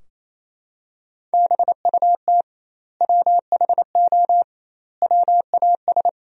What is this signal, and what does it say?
Morse code, heard as a keyed tone, about 28 wpm, 700 Hz. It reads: BUT WHO WAS
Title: 2x Word-Spacing - Sentences from Top 200 Words - 28wpm